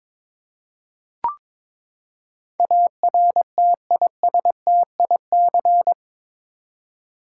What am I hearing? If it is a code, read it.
Morse code, heard as a keyed tone, about 21 wpm, 700 Hz. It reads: ARTISTIC
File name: practice / 22wpm / Eight Letter Words - 22wpm